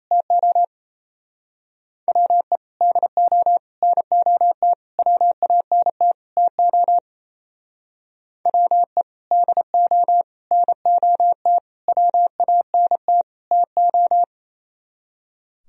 Morse code, heard as a keyed tone, about 34 wpm, 700 Hz. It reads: TO WE DO NOT WANT TO WE DO NOT WANT TO
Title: Speed-Racing - Sets of 5 Words 28wpm